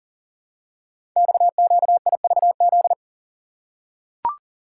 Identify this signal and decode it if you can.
Morse code, heard as a keyed tone, about 40 wpm, 690 Hz. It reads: XQIVZ
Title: Code Groups - 5 Letters - 40wpm